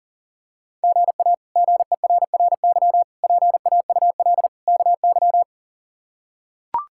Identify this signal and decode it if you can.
Morse code, heard as a keyed tone, about 40 wpm, 700 Hz. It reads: GA GERRY PAUL KY
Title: K1USN SST QSO Practice - 40wpm